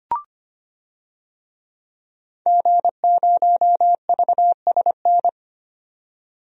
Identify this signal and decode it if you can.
Morse code, heard as a keyed tone, about 25 wpm, 700 Hz. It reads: G0VSN
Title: DXCC-100 Call Signs - 5 Characters - 25wpm